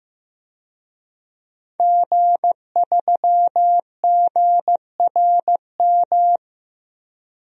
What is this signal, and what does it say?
Morse code, heard as a keyed tone, about 15 wpm, 700 Hz. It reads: G3GRM
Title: DXCC-100 Call Signs - 5 Characters - Repeated 3x - 15wpm